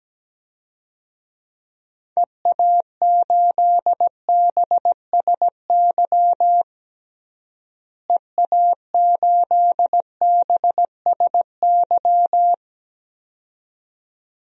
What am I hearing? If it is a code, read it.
Morse code, heard as a keyed tone, about 17 wpm, 700 Hz. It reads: EA8BSY EA8BSY
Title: DXCC-100 Call Signs - 6 Characters - Repeated 2x - 17wpm